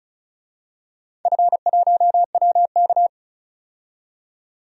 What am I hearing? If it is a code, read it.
Morse code, heard as a keyed tone, about 34 wpm, 700 Hz. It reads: F1WK
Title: DXCC-100 Call Signs - 4 Characters - 35wpm